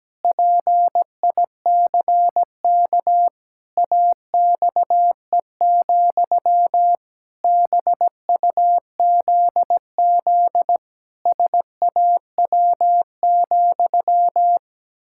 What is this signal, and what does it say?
Morse code, heard as a keyed tone, about 17 wpm, 700 Hz. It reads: PICK AXE, BUZZ SAW,
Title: Pangram Challenge v2 - 17wpm